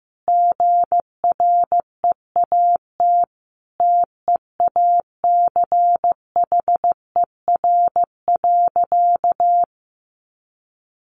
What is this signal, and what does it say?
Morse code, heard as a keyed tone, about 15 wpm, 700 Hz. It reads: GREAT TEACHER.
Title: Sentences from Top 900 Words 15wpm